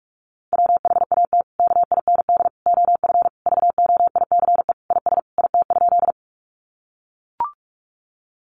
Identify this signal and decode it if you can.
Morse code, heard as a keyed tone, about 44 wpm, 700 Hz. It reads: WHAT KIND OF VOICE IS IT?
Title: Sentences from Top 500 Words 45wpm